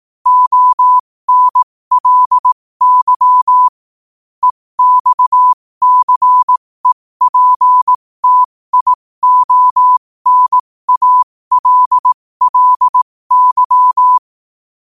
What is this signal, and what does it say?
Morse code, heard as a keyed tone, about 18 wpm, spaced as for 15 wpm, 1000 Hz. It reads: ONLY EXCEPTIONALLY
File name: qotdfiles / 15wpm